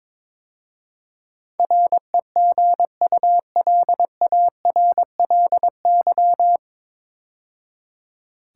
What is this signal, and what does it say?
Morse code, heard as a keyed tone, about 22 wpm, 700 Hz. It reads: REGULARLY
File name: 9 Letter Words - First 3 Letters Given - 22wpm